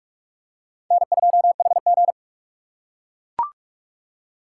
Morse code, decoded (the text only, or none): N2HG